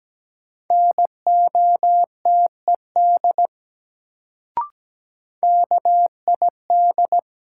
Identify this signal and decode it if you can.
Morse code, heard as a keyed tone, about 17 wpm, 700 Hz. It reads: NOTED KID
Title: Sets of 5 Words - New Word Drill - 17wpm